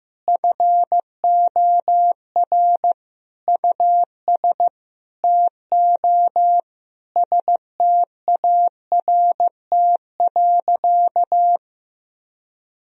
Morse code, heard as a keyed tone, about 15 wpm, 700 Hz. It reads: FOR US TO START.
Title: Sentences from Top 1000 Words - Review Part 4 - 15wpm